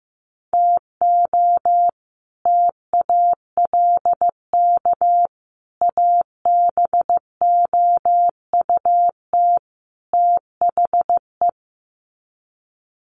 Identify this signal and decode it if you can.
Morse code, heard as a keyed tone, about 15 wpm, 700 Hz. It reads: TO TALK ABOUT THE
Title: Sets of 4 Words 15wpm